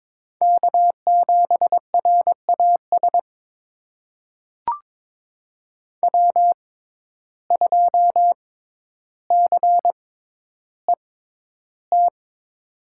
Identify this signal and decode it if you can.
Morse code, heard as a keyed tone, about 22 wpm, spaced as for 11 wpm, 700 Hz. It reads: K7RAS W2CET
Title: USA Call Signs - 6x Character Spacing - 22wpm